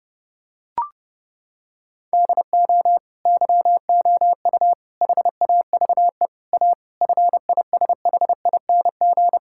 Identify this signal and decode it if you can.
Morse code, heard as a keyed tone, about 30 wpm, 700 Hz. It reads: DO YOU HAVE A FISHING
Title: Sentences from Top 1600 Words 30wpm